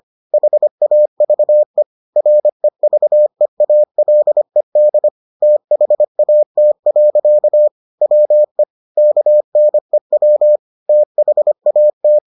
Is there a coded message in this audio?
Yes: Morse code from 0.3 to 12.2 s, heard as a keyed tone, about 25 wpm, 600 Hz. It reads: HAVE REVEALED THAT. WE KNEW THAT